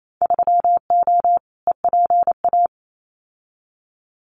Morse code, heard as a keyed tone, about 28 wpm, 700 Hz. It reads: 3O EPA
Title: Winter Field Day Exchanges Encore 28wpm